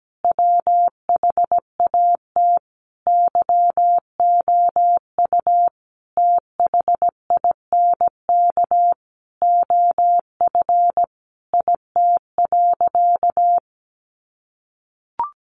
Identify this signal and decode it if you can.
Morse code, heard as a keyed tone, about 17 wpm, 700 Hz. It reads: WHAT YOU THINK OF IT.
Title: Sentences from Top 200 Words 17wpm